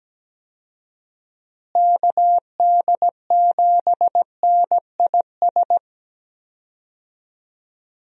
Morse code, heard as a keyed tone, about 17 wpm, 700 Hz. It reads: KD7NIS
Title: USA Call Signs - 6 Characters - Repeated 2x - 17wpm